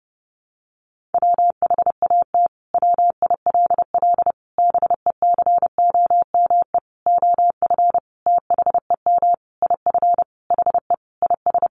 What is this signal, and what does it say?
Morse code, heard as a keyed tone, about 30 wpm, 700 Hz. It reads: WHAT WILL BECOME OF THEM IF HE IS
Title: Sentences from Top 1100 Words 30wpm